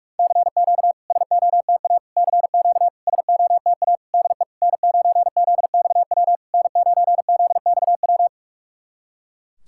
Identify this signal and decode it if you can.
Morse code, heard as a keyed tone, about 48 wpm, 700 Hz. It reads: CQ SOTA CQ SOTA DE N9ZXW N9ZXW
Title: SOTA QSO Practice 45wpm